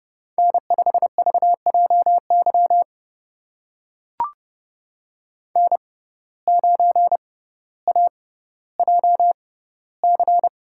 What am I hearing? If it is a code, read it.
Morse code, heard as a keyed tone, about 28 wpm, spaced as for 15 wpm, 700 Hz. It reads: N5VJY N9AJC
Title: USA Call Signs - 6x Character Spacing - 30wpm